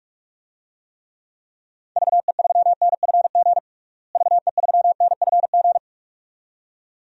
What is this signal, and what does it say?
Morse code, heard as a keyed tone, about 41 wpm, 700 Hz. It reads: VE3NFG VE3NFG
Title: DXCC-100 Call Signs - 6 Characters - Repeated 2x - 45wpm